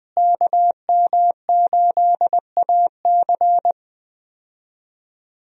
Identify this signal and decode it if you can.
Morse code, heard as a keyed tone, about 21 wpm, 700 Hz. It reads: KM8AC
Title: USA Call Signs - 5 Characters - Repeated 3x - 20wpm